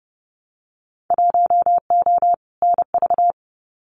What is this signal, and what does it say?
Morse code, heard as a keyed tone, about 30 wpm, 700 Hz. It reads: 1O NV
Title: Winter Field Day Exchanges Encore 30wpm